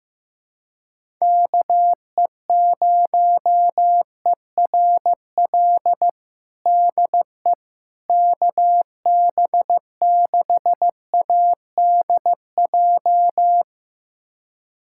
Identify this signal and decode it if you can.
Morse code, heard as a keyed tone, about 15 wpm, 700 Hz. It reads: KE0ERL DE KB6ADJ
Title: USA Callsign DE Callsign 15wpm